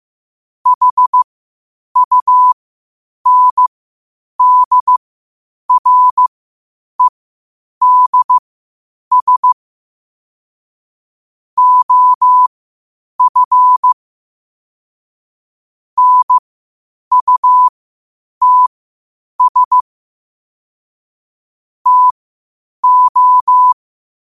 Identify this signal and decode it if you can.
Morse code, heard as a keyed tone, about 15 wpm, spaced as for 8 wpm, 1000 Hz. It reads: HUNDREDS OF NUTS TO
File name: qotdfiles / 8wpm